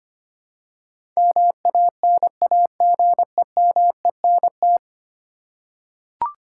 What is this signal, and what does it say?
Morse code, heard as a keyed tone, about 25 wpm, 700 Hz. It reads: MANAGEMENT
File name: Ten Letter Words - 25wpm